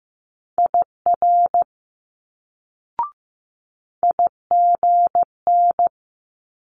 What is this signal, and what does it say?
Morse code, heard as a keyed tone, about 15 wpm, 700 Hz. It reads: IR IGN
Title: Multi-letter Phonemes Quick Fire 15wpm